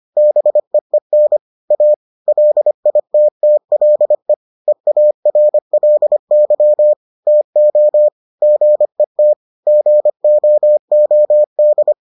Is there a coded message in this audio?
Yes: Morse code from 0.2 to 11.9 s, heard as a keyed tone, about 25 wpm, 600 Hz. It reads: BEEN A LITTLE EARLY TO GET GOOD